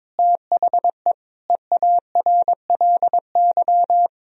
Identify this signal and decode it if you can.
Morse code, heard as a keyed tone, about 22 wpm, 700 Hz. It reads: THE EARLY
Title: Sentences from Top 1000 Words - Review Part 3 - 22wpm